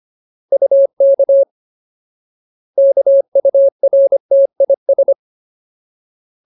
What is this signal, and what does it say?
Morse code, heard as a keyed tone, about 25 wpm, 550 Hz. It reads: UK KURTIS
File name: Ultimate Word List / 25WPM Ultimate